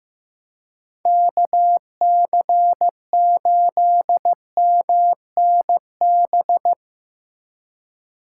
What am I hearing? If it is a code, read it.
Morse code, heard as a keyed tone, about 16 wpm, 700 Hz. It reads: KC8MNB
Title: Speed Racing - USA 2x3 Call Signs 15wpm